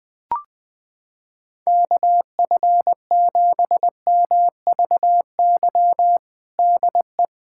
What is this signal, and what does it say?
Morse code, heard as a keyed tone, about 20 wpm, 700 Hz. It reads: KF7MVY DE
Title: USA Callsign DE Callsign - 6 Characters - 20wpm